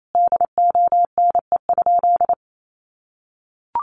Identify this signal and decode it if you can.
Morse code, heard as a keyed tone, about 28 wpm, 700 Hz. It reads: DONE?
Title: Sentences from Top 500 Words 28wpm